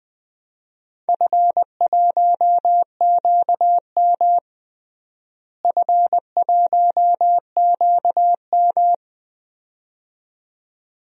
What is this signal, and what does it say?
Morse code, heard as a keyed tone, about 20 wpm, 700 Hz. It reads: F1QM F1QM
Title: DXCC-100 Call Signs - 4 Characters - Repeated 2x - 20wpm